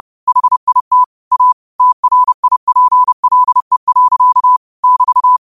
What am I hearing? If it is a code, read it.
Morse code, heard as a keyed tone, about 30 wpm, 1000 Hz. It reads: HIT A TRIPLE. <BT>
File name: qotdfiles / 30wpm